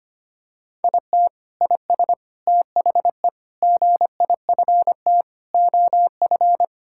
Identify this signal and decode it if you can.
Morse code, heard as a keyed tone, about 25 wpm, 700 Hz. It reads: IT IS THE GIFT OF
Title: Sentences from Top 1000 Words - Review Part 6 - 25wpm